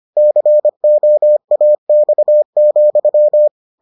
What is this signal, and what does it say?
Morse code, heard as a keyed tone, about 25 wpm, 600 Hz. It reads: COAX,